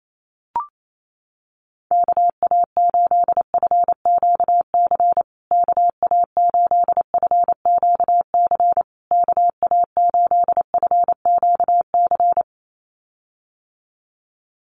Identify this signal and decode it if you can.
Morse code, heard as a keyed tone, about 28 wpm, 700 Hz. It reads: KA8FQC KA8FQC KA8FQC
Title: USA Call Signs Repeated 3x 28wpm